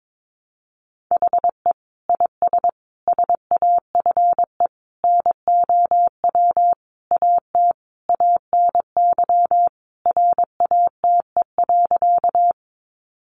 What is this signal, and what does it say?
Morse code, heard as a keyed tone, about 22 wpm, 700 Hz. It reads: HE IS SAFE NOW AT ANY RATE.
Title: Sentences from Top 900 Words 22wpm